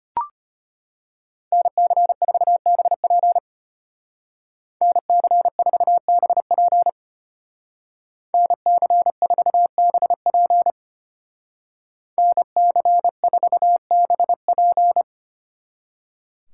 Morse code, heard as a keyed tone, about 33 wpm, 700 Hz. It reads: NC4BP NC4BP NC4BP NC4BP